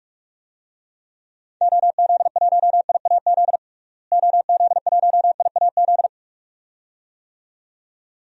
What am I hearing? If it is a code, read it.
Morse code, heard as a keyed tone, about 43 wpm, 700 Hz. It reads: OZ1IAZ OZ1IAZ
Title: DXCC-100 Call Signs - 6 Characters - Repeated 2x - 45wpm